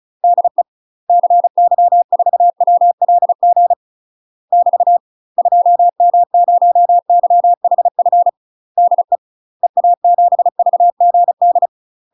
Morse code, heard as a keyed tone, about 36 wpm, 700 Hz. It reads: DE CY4WLG <BT> 2M0YHF DE EA7VGD
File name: UK MASTER QSOs TRAINING / 35WPM / UK MASTER QSOs 0033